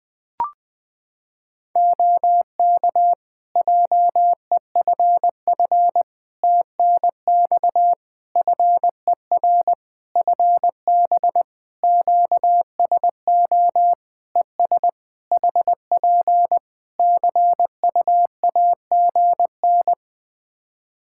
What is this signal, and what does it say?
Morse code, heard as a keyed tone, about 20 wpm, 700 Hz. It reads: OK JEFF TNX FER FB QSO ES HP CUAGN
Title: Practice QSO - Exchange 4 - Ending - 20wpm